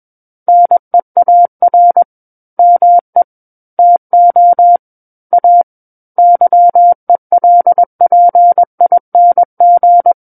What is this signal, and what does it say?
Morse code, heard as a keyed tone, about 21 wpm, 700 Hz. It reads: NEAR ME TO A YELPING
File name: Book One / 12-Chapter 12 - 21 WPM